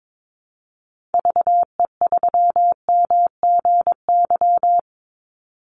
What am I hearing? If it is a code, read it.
Morse code, heard as a keyed tone, about 22 wpm, 700 Hz. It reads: VE3MGY
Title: Call Signs Repeated 3x - 22wpm